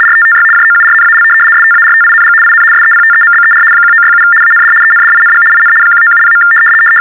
RTTY, about 75 baud. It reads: RTTY JOURNAL STARTING IN SEPTEMBER 1967 AND ENDING IN 1970.
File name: teletype-RTTY-75